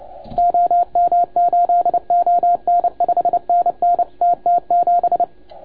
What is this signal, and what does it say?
Morse code, heard as a keyed tone, about 29 wpm, 690 Hz. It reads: OM8ON 5NN TT7
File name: Archiv / CUC287 / OK1FGD CUC 287